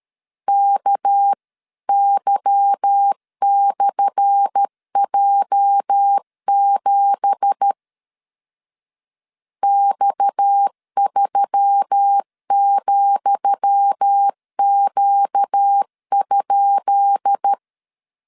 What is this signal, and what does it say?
Morse code, heard as a keyed tone, about 13 wpm, 800 Hz. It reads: KY/J7 X3,Q?
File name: LEZIONI / 10 SIGNS / 10D COMA 12wpm 2544